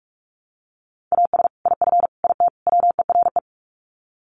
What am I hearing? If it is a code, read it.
Morse code, heard as a keyed tone, about 44 wpm, 700 Hz. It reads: AS IF IT WERE
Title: Sets of 4 Words 45wpm